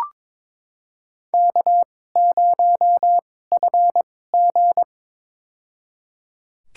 Morse code, heard as a keyed tone, about 22 wpm, spaced as for 16 wpm, 700 Hz. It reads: K0FG
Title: USA Call Signs - 4 Characters - 2x Character Spacing - 22wpm